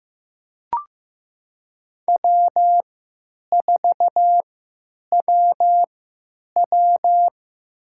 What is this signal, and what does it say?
Morse code, heard as a keyed tone, about 15 wpm, spaced as for 8 wpm, 700 Hz. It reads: W4WW
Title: USA Call Signs - 4 Characters - 3x Character Spacing - 15wpm